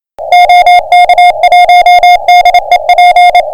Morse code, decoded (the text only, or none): OK1DEP